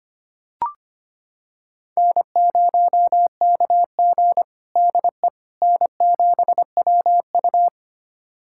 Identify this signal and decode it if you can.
Morse code, heard as a keyed tone, about 25 wpm, 700 Hz. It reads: N0KG DE N7WU
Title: USA Callsign DE Callsign - 4 Characters - 25wpm